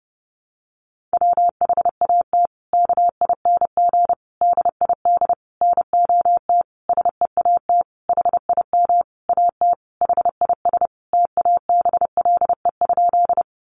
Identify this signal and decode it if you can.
Morse code, heard as a keyed tone, about 30 wpm, 700 Hz. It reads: WHAT KING DID NOT SEAT HIM AT HIS TABLE?
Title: Sentences from Top 1500 Words 30wpm